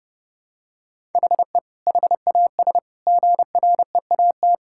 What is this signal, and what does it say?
Morse code, heard as a keyed tone, about 30 wpm, 700 Hz. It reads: HE HAS GREAT
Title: Sentences from Top 1000 Words - Review Part 6 - 30wpm